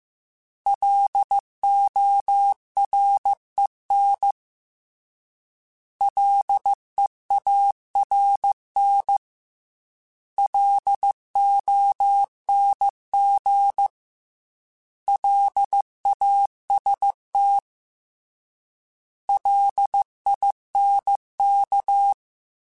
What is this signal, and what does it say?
Morse code, heard as a keyed tone, about 15 wpm, 800 Hz. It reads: LOREN LEARN LONG LAST LINK